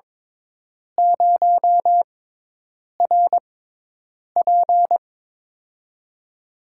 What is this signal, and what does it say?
Morse code, heard as a keyed tone, about 23 wpm, spaced as for 8 wpm, 700 Hz. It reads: 0RP
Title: USA Call Signs - 6x Character Spacing - 22wpm